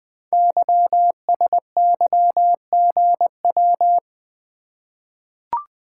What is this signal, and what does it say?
Morse code, heard as a keyed tone, about 20 wpm, 700 Hz. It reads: YSYGW
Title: Code Groups - 5 Letters - 20wpm